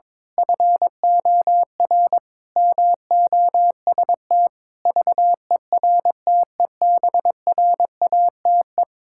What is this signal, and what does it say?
Morse code, heard as a keyed tone, about 22 wpm, 700 Hz. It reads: FOR MOST VERTEBRATE
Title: Pangram Challenge v2 - 22wpm